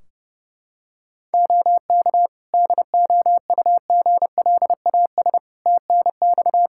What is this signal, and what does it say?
Morse code, heard as a keyed tone, about 32 wpm, 700 Hz. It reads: OK DOUGLAS TNX
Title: Practice QSO - Exchange 4 - Ending - 30wpm